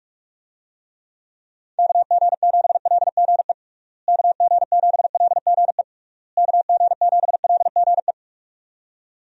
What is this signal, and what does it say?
Morse code, heard as a keyed tone, about 43 wpm, 700 Hz. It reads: KG7LGE KG7LGE KG7LGE
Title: USA Call Signs - 6 Characters - Repeated 3x - 45wpm